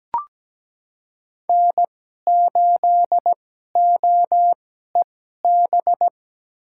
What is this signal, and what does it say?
Morse code, heard as a keyed tone, about 17 wpm, spaced as for 12 wpm, 700 Hz. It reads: N8OEB
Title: USA Call Signs - 5 Characters - 2x Character Spacing - 17wpm